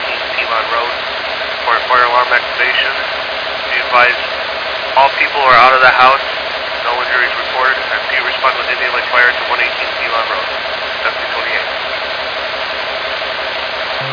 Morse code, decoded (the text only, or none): none